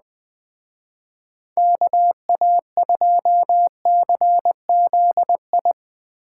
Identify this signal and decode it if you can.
Morse code, heard as a keyed tone, about 20 wpm, 700 Hz. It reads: KA2CZI
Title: Speed Racing - USA 2x3 Call Signs 17wpm